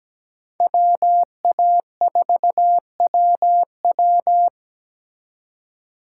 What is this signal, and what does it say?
Morse code, heard as a keyed tone, about 20 wpm, 700 Hz. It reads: WA4WW
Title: Speed Racing - USA 2x2 Call Signs 17wpm